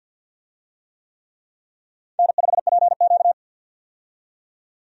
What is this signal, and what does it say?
Morse code, heard as a keyed tone, about 51 wpm, 700 Hz. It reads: N5PQ
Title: USA Call Signs - 4 Characters - 50wpm